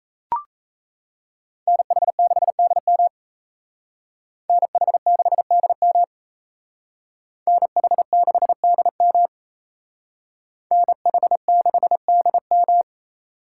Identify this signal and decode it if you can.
Morse code, heard as a keyed tone, about 34 wpm, 700 Hz. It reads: NH6DM NH6DM NH6DM NH6DM
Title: Speed Racing - USA 2x2 Call Signs 28wpm